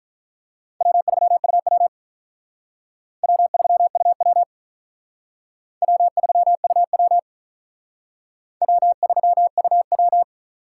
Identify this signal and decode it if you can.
Morse code, heard as a keyed tone, about 45 wpm, 700 Hz. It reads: W3UW W3UW W3UW W3UW